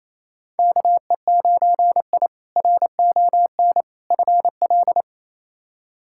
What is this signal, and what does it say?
Morse code, heard as a keyed tone, about 28 wpm, 700 Hz. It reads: KE9I RON FL
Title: K1USN SST QSO Practice - 28wpm